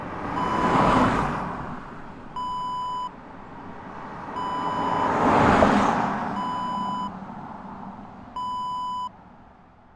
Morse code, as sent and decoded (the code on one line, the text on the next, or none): none